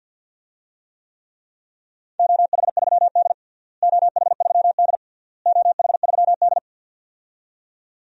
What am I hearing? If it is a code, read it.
Morse code, heard as a keyed tone, about 53 wpm, 700 Hz. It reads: OH3D OH3D OH3D
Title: DXCC-100 Call Signs - 4 Characters - Repeated 3x - 50wpm